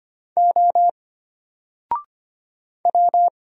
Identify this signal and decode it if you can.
Morse code, heard as a keyed tone, about 25 wpm, 700 Hz. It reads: O W
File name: Single-Letter-Number-25wpm